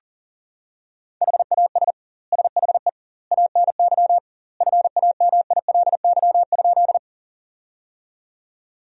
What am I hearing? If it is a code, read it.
Morse code, heard as a keyed tone, about 40 wpm, 700 Hz. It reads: HAS SHE ANY FAMILY?
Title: 2x Word-Spacing - Sentences from Top 400 Words - 40wpm